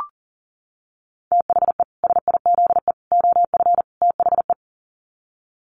Morse code, heard as a keyed tone, about 40 wpm, 700 Hz. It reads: THE SIZE OF THE
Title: Sets of 4 Words 40wpm